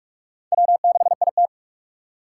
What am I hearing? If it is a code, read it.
Morse code, heard as a keyed tone, about 60 wpm, spaced as for 53 wpm, 700 Hz. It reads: W6IT
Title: Speed Racing - USA 1x2 Call Signs 45wpm